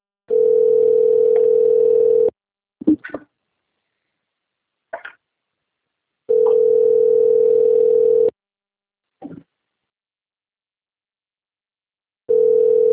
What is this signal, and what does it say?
Ringback tone (North America), 440 + 480 Hz.
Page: Limeira (Frente)